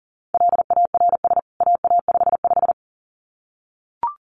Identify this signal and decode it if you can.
Morse code, heard as a keyed tone, about 40 wpm, 700 Hz. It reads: LARS AA55
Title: CWT Exchange 40wpm